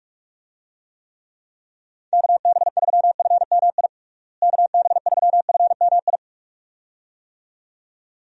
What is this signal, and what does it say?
Morse code, heard as a keyed tone, about 41 wpm, 700 Hz. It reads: KB3FMI KB3FMI
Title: USA Call Signs - 6 Characters - Repeated 2x - 45wpm